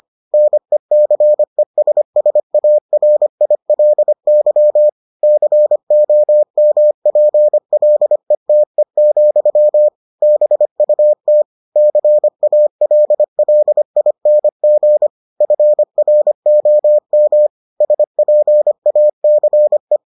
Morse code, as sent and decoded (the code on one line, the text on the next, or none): -. -.-. ... ... .- .-. .. .-.. -.-- / -.-. --- -- .--. .-.. - . --..-- / -... ..- - / -.-. .- .-.. .-.. .. -. --. / ..-. .-. --- -- / ... .--. .- -.-. .
NECESSARILY COMPLETE, BUT CALLING FROM SPACE